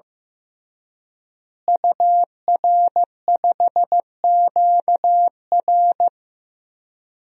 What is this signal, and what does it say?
Morse code, heard as a keyed tone, about 15 wpm, 700 Hz. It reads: UR5QR
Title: DXCC-100 Call Signs - 5 Characters - Repeated 3x - 15wpm